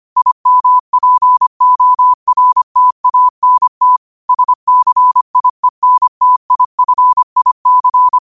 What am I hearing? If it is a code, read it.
Morse code, heard as a keyed tone, about 25 wpm, 1000 Hz. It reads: IMPORTANT SCIENTIFIC
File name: qotdfiles / 25wpm